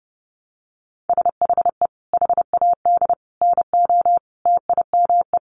Morse code, heard as a keyed tone, about 28 wpm, 700 Hz. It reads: SHE HAD NO TIME
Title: Sentences from Top 1100 Words 30wpm